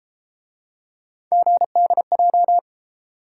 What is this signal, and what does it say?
Morse code, heard as a keyed tone, about 33 wpm, 700 Hz. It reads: GDJ